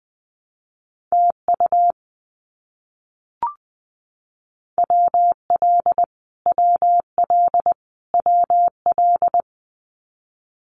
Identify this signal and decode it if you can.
Morse code, heard as a keyed tone, about 20 wpm, 700 Hz. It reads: TU WL WL WL
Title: Abbreviations Repeated 3x - 20wpm